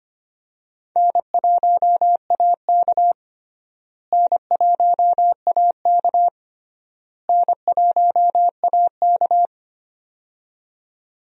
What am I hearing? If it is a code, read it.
Morse code, heard as a keyed tone, about 25 wpm, 700 Hz. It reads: N1AK N1AK N1AK
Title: USA Call Signs - 4 Characters - Repeated 3x - 25wpm